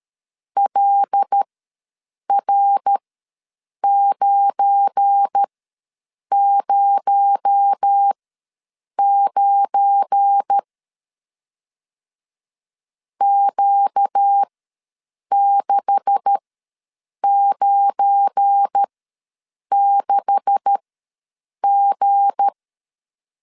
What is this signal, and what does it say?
Morse code, heard as a keyed tone, about 13 wpm, spaced as for 7 wpm, 800 Hz. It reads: LR909 Q696G
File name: LEZIONI / 9 286 ws 12wpm 3351